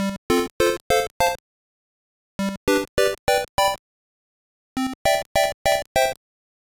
There is no code.